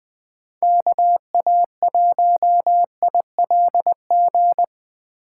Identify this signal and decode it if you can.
Morse code, heard as a keyed tone, about 20 wpm, 700 Hz. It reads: KA1ILG
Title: Speed Racing - USA 2x3 Call Signs 17wpm